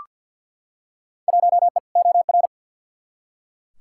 Morse code, heard as a keyed tone, about 52 wpm, 700 Hz. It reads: 1E OR
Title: Field Day Exchanges Encore - 50wpm